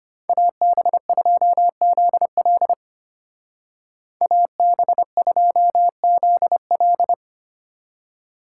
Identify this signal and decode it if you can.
Morse code, heard as a keyed tone, about 29 wpm, 700 Hz. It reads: AB2ZL AB2ZL
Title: Speed Racing - USA 2x2 Call Signs 25wpm